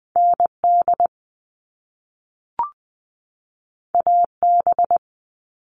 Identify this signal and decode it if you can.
Morse code, heard as a keyed tone, about 20 wpm, 700 Hz. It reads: ND AB
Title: All Cut Numbers - 2-Digit 20wpm